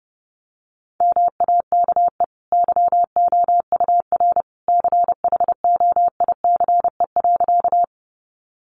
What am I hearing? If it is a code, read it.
Morse code, heard as a keyed tone, about 30 wpm, 700 Hz. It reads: MAKE YOUR CHOICE.
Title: Sentences from Top 900 Words 30wpm